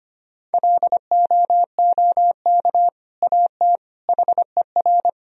Morse code, heard as a keyed tone, about 25 wpm, 700 Hz. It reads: LOOK AT HER
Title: Sentences from Top 1000 Words - Review Part 1 - 25wpm